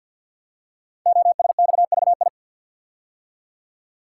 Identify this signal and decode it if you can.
Morse code, heard as a keyed tone, about 48 wpm, 690 Hz. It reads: OSXVI
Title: Code Groups - 5 Letters - 50wpm